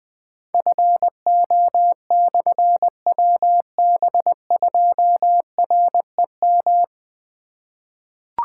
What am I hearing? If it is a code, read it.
Morse code, heard as a keyed tone, about 20 wpm, 700 Hz. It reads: FO/WB2REM
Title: Difficult Call Signs 20wpm